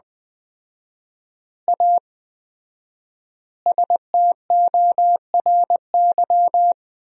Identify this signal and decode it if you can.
Morse code, heard as a keyed tone, about 21 wpm, 700 Hz. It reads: A STORY
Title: 4x Word-Spacing - Sentences from Top 500 Words - 20wpm